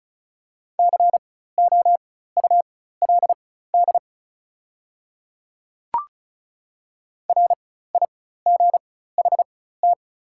Morse code, heard as a keyed tone, about 36 wpm, spaced as for 16 wpm, 700 Hz. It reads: COULD RIGHT
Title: 4x Character Spacing - Top 200 Words - 35wpm